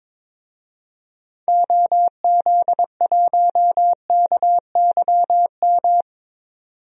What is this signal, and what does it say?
Morse code, heard as a keyed tone, about 22 wpm, 700 Hz. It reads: OZ1KYM
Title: DXCC-100 Call Signs - 6 Characters - 22wpm